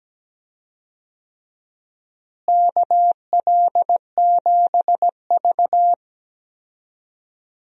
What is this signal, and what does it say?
Morse code, heard as a keyed tone, about 17 wpm, 700 Hz. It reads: KL7V
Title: DXCC-100 Call Signs - 4 Characters - Repeated 2x - 17wpm